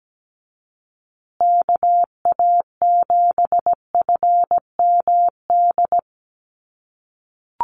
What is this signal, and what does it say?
Morse code, heard as a keyed tone, about 17 wpm, 700 Hz. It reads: KA7FMD
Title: USA Call Signs Repeated 2x 17wpm